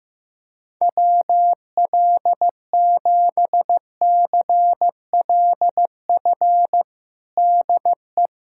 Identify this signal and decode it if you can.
Morse code, heard as a keyed tone, about 16 wpm, 700 Hz. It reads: WL7CLF DE
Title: USA Callsign DE Callsign - 6 Characters - 15wpm